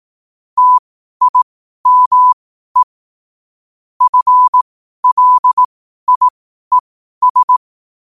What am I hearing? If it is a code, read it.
Morse code, heard as a keyed tone, about 18 wpm, spaced as for 12 wpm, 1000 Hz. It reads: TIME FLIES
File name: qotdfiles / 12wpm